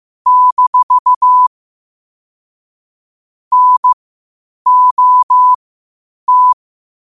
Morse code, heard as a keyed tone, about 15 wpm, spaced as for 8 wpm, 1000 Hz. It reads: - NOT
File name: qotdfiles / 8wpm